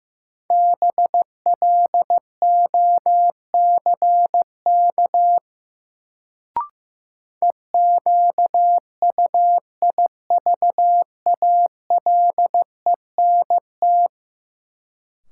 Morse code, heard as a keyed tone, about 15 wpm, 700 Hz. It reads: BLOCK EQUIVALENT
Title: Sentences from Top 1600 Words - New Word Drill - 15wpm